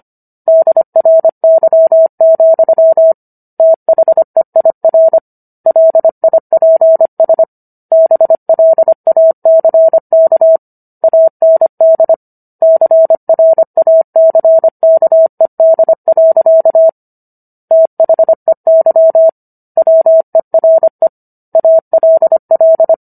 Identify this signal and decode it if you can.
Morse code, heard as a keyed tone, about 25 wpm, 650 Hz. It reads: DRY, THEIR LIPS BLACK AND CRACKED. THEY WERE ALL